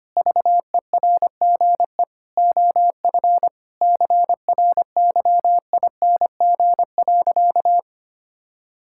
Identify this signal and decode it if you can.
Morse code, heard as a keyed tone, about 25 wpm, 700 Hz. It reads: VERGE OF CRYING.
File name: practice / 25wpm / Sentences with 4-Word Phrases - 25wpm